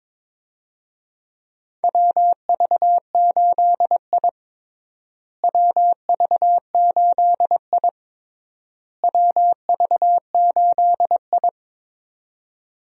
Morse code, heard as a keyed tone, about 22 wpm, 700 Hz. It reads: WV8I WV8I WV8I
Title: USA Call Signs - 4 Characters - Repeated 3x - 22wpm